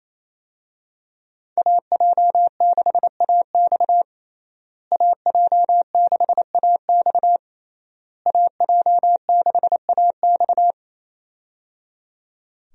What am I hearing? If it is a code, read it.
Morse code, heard as a keyed tone, about 27 wpm, 700 Hz. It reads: AJ6AX AJ6AX AJ6AX
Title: USA Call Signs - 5 Characters - Repeated 3x - 28wpm